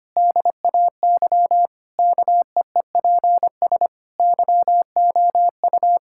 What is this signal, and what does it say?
Morse code, heard as a keyed tone, about 25 wpm, 700 Hz. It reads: DAY KEEPS YOU
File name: practice / 25wpm / Difficult Sentences with Hints - 25wpm